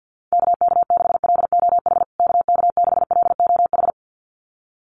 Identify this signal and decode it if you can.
Morse code, heard as a keyed tone, about 50 wpm, 700 Hz. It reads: KK6LOH KK6LOH
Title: USA Call Signs Repeated 2x 50wpm